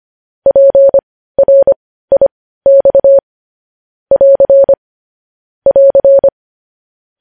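Morse code, heard as a keyed tone, about 25 wpm, spaced as for 15 wpm, 550 Hz. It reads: PRIX <AR> <AR>
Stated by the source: Morse news headlines practice sent at 15 WPM.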